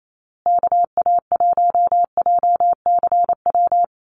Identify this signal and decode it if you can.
Morse code, heard as a keyed tone, about 28 wpm, 700 Hz. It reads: KA1JCW
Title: USA Call Signs Repeated 2x 28wpm